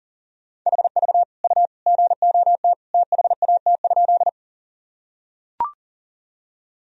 Morse code, heard as a keyed tone, about 40 wpm, 700 Hz. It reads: HV U GOT THAT?
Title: Sentences from Top 300 Words and Abbreviations 40wpm